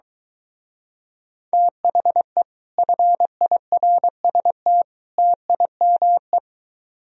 Morse code, heard as a keyed tone, about 22 wpm, 700 Hz. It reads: THE FIRST TIME